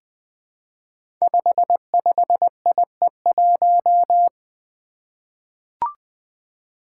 Morse code, heard as a keyed tone, about 20 wpm, 700 Hz. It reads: 55IE1
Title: Five Challenging Letter-Number Combinations 20wpm